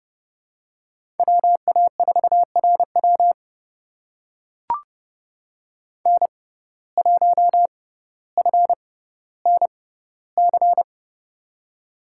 Morse code, heard as a keyed tone, about 31 wpm, spaced as for 16 wpm, 700 Hz. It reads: WA4RW N1FNC
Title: USA Call Signs - 5 Characters - 6x Character Spacing - 30wpm